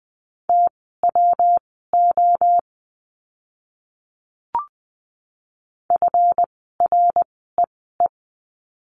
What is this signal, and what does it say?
Morse code, heard as a keyed tone, about 20 wpm, spaced as for 15 wpm, 700 Hz. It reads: TWO FREE
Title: 139 Lesson 23f - Letter f - Words 20wpm